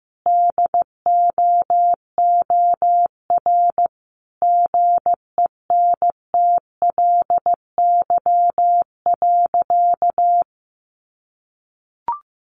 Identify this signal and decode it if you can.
Morse code, heard as a keyed tone, about 15 wpm, 700 Hz. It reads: DOOR GENTLY.
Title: Sentences from Top 1500 Words 15wpm